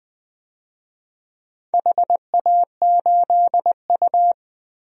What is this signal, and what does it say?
Morse code, heard as a keyed tone, about 21 wpm, 700 Hz. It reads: HA8U
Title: DXCC-100 Call Signs - 4 Characters - Repeated 3x - 20wpm